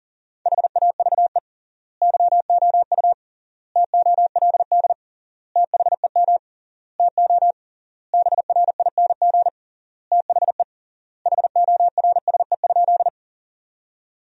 Morse code, heard as a keyed tone, about 40 wpm, 700 Hz. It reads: HAVE YOU TOLD THEM TO BRING THE HORSE?
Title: 3x Word-Spacing - Sentences from Top 500 Words - 40wpm